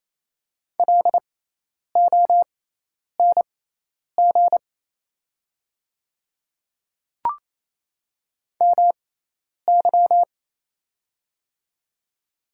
Morse code, heard as a keyed tone, about 28 wpm, spaced as for 10 wpm, 700 Hz. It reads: LONG MY
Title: 6x Character Spacing - Top 100 Words - 28wpm